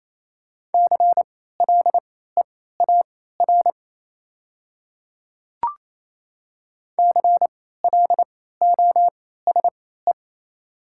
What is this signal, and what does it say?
Morse code, heard as a keyed tone, about 27 wpm, spaced as for 16 wpm, 700 Hz. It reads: CLEAR CLOSE
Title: 3x Character Spacing - Top 500 Words - 28wpm